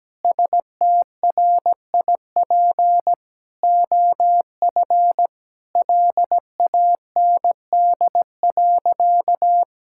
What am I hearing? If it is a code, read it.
Morse code, heard as a keyed tone, about 17 wpm, 700 Hz. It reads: STRIP OF LAND.
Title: Sentences from Top 1600 Words 17wpm